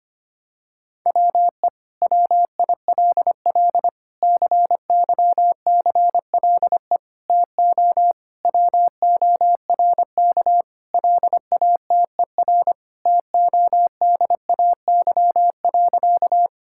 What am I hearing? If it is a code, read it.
Morse code, heard as a keyed tone, about 25 wpm, 700 Hz. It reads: WE WILL CYCLE TO WORK LATER TODAY.
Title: Sentences from Top 1000 Words - Review Part 6 - 25wpm